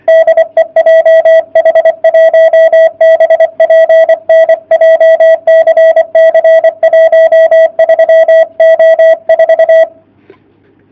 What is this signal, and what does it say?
Morse code, heard as a keyed tone, about 24 wpm, 660 Hz. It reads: DE JH1BPN JCC 13O4
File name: jh8cbh / 2019cw / sp31- w